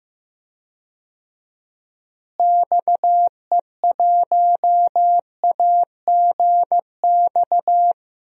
Morse code, heard as a keyed tone, about 15 wpm, 700 Hz. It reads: XE1AGX
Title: DXCC-100 Call Signs - 6 Characters - Repeated 2x - 15wpm